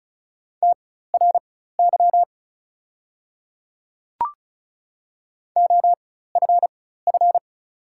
Morse code, heard as a keyed tone, about 36 wpm, spaced as for 16 wpm, 690 Hz. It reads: TRY OFF
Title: 4x Character Spacing - Top 500 Words - 35wpm